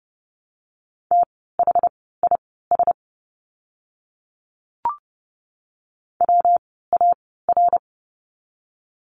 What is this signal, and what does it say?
Morse code, heard as a keyed tone, about 30 wpm, spaced as for 17 wpm, 700 Hz. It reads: THIS WAR
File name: course / 30:10wpm / 115 Lesson 19f - Letter w - Words 30wpm